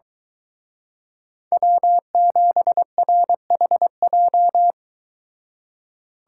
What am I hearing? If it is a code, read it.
Morse code, heard as a keyed tone, about 23 wpm, 700 Hz. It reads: W7RHJ